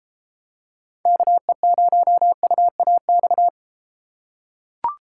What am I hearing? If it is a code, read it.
Morse code, heard as a keyed tone, about 32 wpm, 700 Hz. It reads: KE0UAX